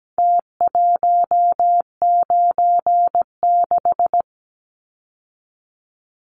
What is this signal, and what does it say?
Morse code, heard as a keyed tone, about 17 wpm, 700 Hz. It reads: T196
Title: Numbers - Contest Style - 4-Digit 17wpm